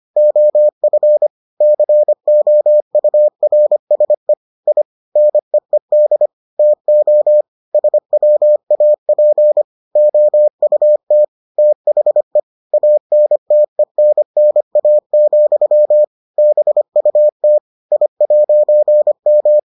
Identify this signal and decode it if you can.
Morse code, heard as a keyed tone, about 25 wpm, 600 Hz. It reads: OF COURSE I NEED TO SWAP OUT THE ANTENNA, BUT I'M